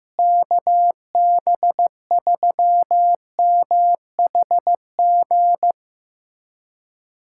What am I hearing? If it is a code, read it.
Morse code, heard as a keyed tone, about 14 wpm, 700 Hz. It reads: KB3MHG
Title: USA Callsign DE Callsign 15wpm